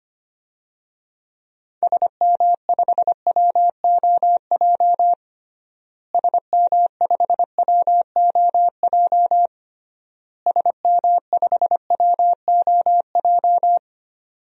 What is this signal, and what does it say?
Morse code, heard as a keyed tone, about 25 wpm, 700 Hz. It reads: SM5WOJ SM5WOJ SM5WOJ
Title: DXCC-100 Call Signs - 6 Characters - Repeated 3x - 25wpm